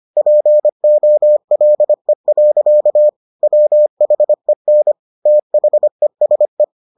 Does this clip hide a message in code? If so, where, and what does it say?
Yes: Morse code from 0.2 to 6.6 s, heard as a keyed tone, about 25 wpm, 600 Hz. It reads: POLE. WHEN THESE